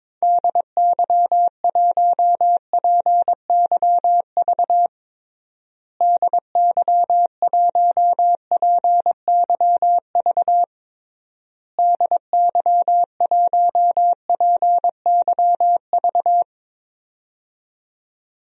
Morse code, heard as a keyed tone, about 22 wpm, 700 Hz. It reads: DY1PYV DY1PYV DY1PYV
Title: DXCC-100 Call Signs - 6 Characters - Repeated 3x - 22wpm